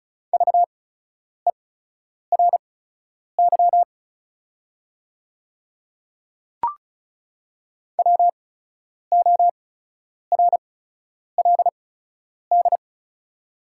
Morse code, heard as a keyed tone, about 35 wpm, spaced as for 10 wpm, 690 Hz. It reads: VERY WORLD
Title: 8x Character Spacing - Top 200 Words - 35wpm